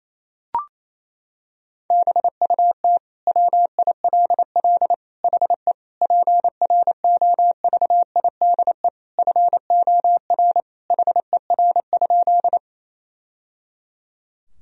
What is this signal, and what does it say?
Morse code, heard as a keyed tone, about 28 wpm, 700 Hz. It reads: BUT WILL HE PROVIDE FOR HER?
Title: Sentences from Top 1000 Words - Review Part 3 - 28wpm